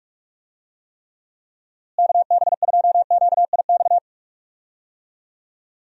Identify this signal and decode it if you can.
Morse code, heard as a keyed tone, about 41 wpm, 700 Hz. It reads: KB2QIX
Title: USA Call Signs - 6 Characters - Repeated 3x - 45wpm